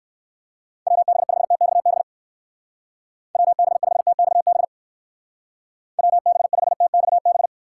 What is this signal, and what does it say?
Morse code, heard as a keyed tone, about 56 wpm, 700 Hz. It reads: WB5TXB WB5TXB WB5TXB